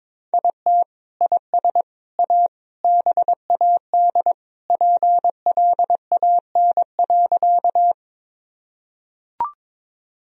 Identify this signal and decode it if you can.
Morse code, heard as a keyed tone, about 22 wpm, 700 Hz. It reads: IT IS A BAD PLAN.
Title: Sentences from Top 1000 Words - Review Part 2 - 22wpm